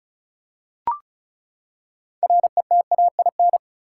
Morse code, heard as a keyed tone, about 36 wpm, 700 Hz. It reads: RETAIN
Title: Six Letter Words - 35wpm